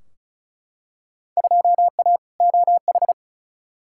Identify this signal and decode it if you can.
Morse code, heard as a keyed tone, about 36 wpm, 700 Hz. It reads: 2A OH
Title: Field Day Exchanges - 35wpm